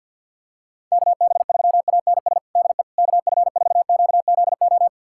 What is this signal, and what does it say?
Morse code, heard as a keyed tone, about 48 wpm, 690 Hz. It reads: KB3ANS DE KF4QZO
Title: USA Callsign DE Callsign - 6 Characters - 50wpm